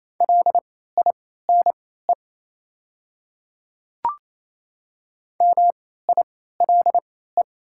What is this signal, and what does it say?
Morse code, heard as a keyed tone, about 28 wpm, spaced as for 16 wpm, 700 Hz. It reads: LINE MILE
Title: 3x Character Spacing - Top 300 Words - 28wpm